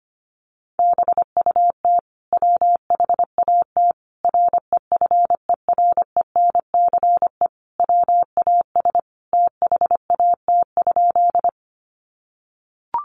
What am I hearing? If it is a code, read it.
Morse code, heard as a keyed tone, about 25 wpm, 700 Hz. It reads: BUT WHAT REFERENCE WAS THAT?
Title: Sentences from Top 1200 Words 25wpm